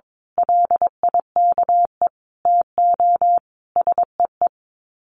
Morse code, heard as a keyed tone, about 22 wpm, 700 Hz. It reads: LIKE TO SEE
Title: Sets of 4 Words 22wpm